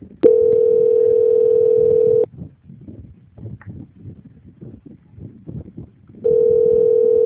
Ringback tone (North America), 440 + 480 Hz.